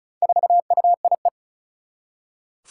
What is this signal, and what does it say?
Morse code, heard as a keyed tone, about 34 wpm, 700 Hz. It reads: 4UIE
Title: Four Challenging Letter-Number Combinations 35wpm